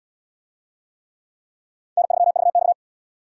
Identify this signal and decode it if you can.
Morse code, heard as a keyed tone, about 77 wpm, 700 Hz. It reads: N3VX